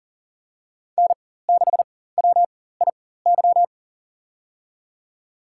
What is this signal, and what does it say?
Morse code, heard as a keyed tone, about 40 wpm, spaced as for 19 wpm, 700 Hz. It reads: N6WIY
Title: USA Call Signs - 5 Characters - 4x Character Spacing - 40wpm